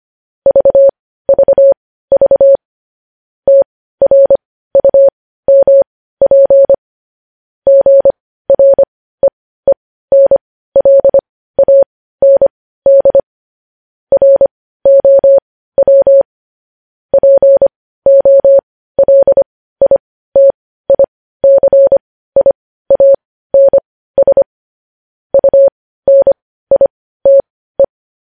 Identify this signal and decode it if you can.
Morse code, heard as a keyed tone, about 25 wpm, spaced as for 15 wpm, 550 Hz. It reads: VVV TRUMP GREENLAND ROW POLITICIANS UNITE
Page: news headlines in morse code at 15 wpm